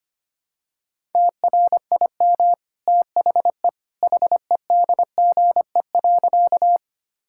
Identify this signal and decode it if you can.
Morse code, heard as a keyed tone, about 25 wpm, 700 Hz. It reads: TRIM THE HEDGE.
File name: Easy Sentences with Hints - 25wpm